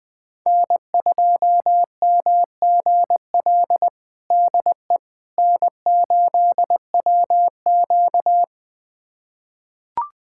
Morse code, heard as a keyed tone, about 20 wpm, 700 Hz. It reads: N2MGL DE N8WQ
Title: USA Callsign DE Callsign 20wpm